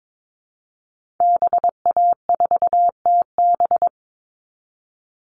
Morse code, heard as a keyed tone, about 22 wpm, 700 Hz. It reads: BA4TB
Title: Call Signs Repeated 3x - 22wpm